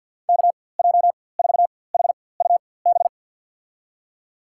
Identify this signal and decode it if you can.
Morse code, heard as a keyed tone, about 53 wpm, spaced as for 24 wpm, 700 Hz. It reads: KJ4HUB
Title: USA Call Signs - 6 Characters - 4x Character Spacing - 50wpm